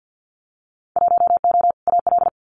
Morse code, heard as a keyed tone, about 50 wpm, 700 Hz. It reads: 1O AL
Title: Winter Field Day Exchanges 50wpm